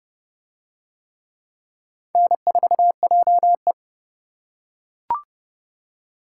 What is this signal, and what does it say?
Morse code, heard as a keyed tone, about 27 wpm, 700 Hz. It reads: N4JE